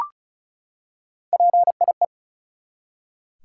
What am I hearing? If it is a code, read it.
Morse code, heard as a keyed tone, about 34 wpm, 700 Hz. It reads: PIE
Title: Three Letter Words - 35wpm